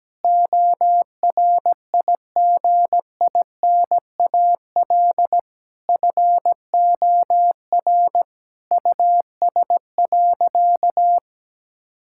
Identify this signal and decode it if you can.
Morse code, heard as a keyed tone, about 17 wpm, 700 Hz. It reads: ORIGINAL FOR US.
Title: Sentences from Top 1000 Words - Review Part 6 - 17wpm